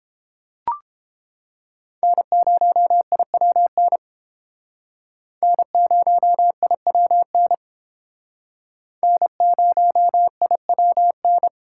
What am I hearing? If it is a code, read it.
Morse code, heard as a keyed tone, about 31 wpm, 700 Hz. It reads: N0IWN N0IWN N0IWN